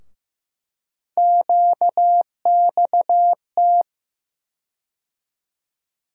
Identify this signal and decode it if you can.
Morse code, heard as a keyed tone, about 15 wpm, 700 Hz. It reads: QXT
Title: USA Call Sign Postfixes - 3 Letters 15wpm